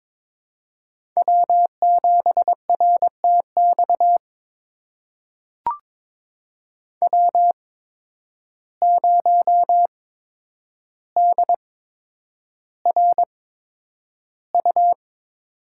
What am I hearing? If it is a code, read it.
Morse code, heard as a keyed tone, about 22 wpm, spaced as for 9 wpm, 700 Hz. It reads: W7RTX W0DRU
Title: USA Call Signs - 5 Characters - 8x Character Spacing - 22wpm